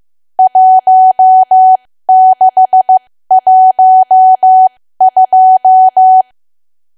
Morse code, heard as a keyed tone, about 15 wpm, spaced as for 13 wpm, 750 Hz. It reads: 1612
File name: cwa / W1AW-Level2 / 13WPM